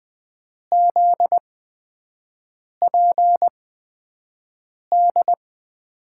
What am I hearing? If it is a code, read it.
Morse code, heard as a keyed tone, about 20 wpm, spaced as for 5 wpm, 700 Hz. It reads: ZPD
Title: USA Call Signs - 6 Characters - 8x Character Spacing - 20wpm